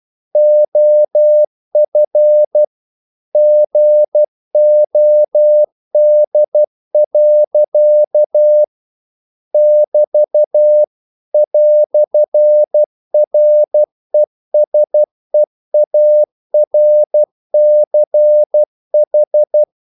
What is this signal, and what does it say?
Morse code, heard as a keyed tone, about 12 wpm, 600 Hz. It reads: OF GOD. <BT> "RESEARCH
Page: English quotations in CW (telegraphy - morse)